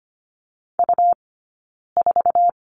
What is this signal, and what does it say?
Morse code, heard as a keyed tone, about 25 wpm, 700 Hz. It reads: U 4
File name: course / 25:10wpm / 099 Lesson 17d - Letter u - Two Characters 25wpm